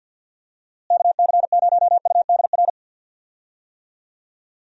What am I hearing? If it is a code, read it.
Morse code, heard as a keyed tone, about 52 wpm, 700 Hz. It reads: KC0UDR
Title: USA Call Signs - 6 Characters - Repeated 2x - 50wpm